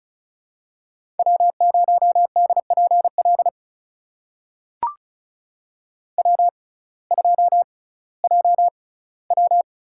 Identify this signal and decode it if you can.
Morse code, heard as a keyed tone, about 34 wpm, spaced as for 19 wpm, 700 Hz. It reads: W0DPL W2JW
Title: USA Call Signs - 5 Characters - 6x Character Spacing - 35wpm